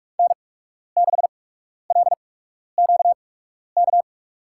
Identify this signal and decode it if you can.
Morse code, heard as a keyed tone, about 44 wpm, spaced as for 12 wpm, 700 Hz. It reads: N6LQK
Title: USA Call Signs - 5 Characters - 8x Character Spacing - 45wpm